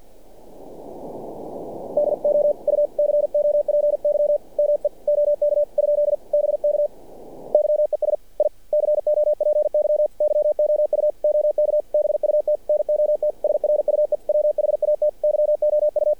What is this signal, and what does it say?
Morse code, heard as a keyed tone, about 47 wpm, 600 Hz. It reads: DO U COPY ME OK ? BK YES I COPY YOU OK BUT NOT SRUE WHAT YOU
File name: archived / CRSnet